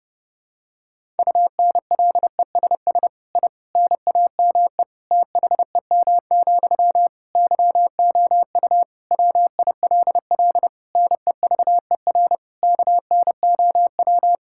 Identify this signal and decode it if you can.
Morse code, heard as a keyed tone, about 30 wpm, 700 Hz. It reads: UNLESS I NAME THEM, YOU WILL NEVER KNOW
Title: Sentences from Top 1000 Words - Review Part 6 - 30wpm